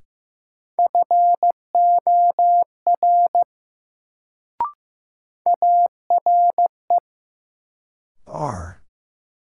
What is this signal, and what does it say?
Morse code, heard as a keyed tone, about 15 wpm, 700 Hz. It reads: FOR ARE
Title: Sentences from Top 100 Words - New Word Drill - 15wpm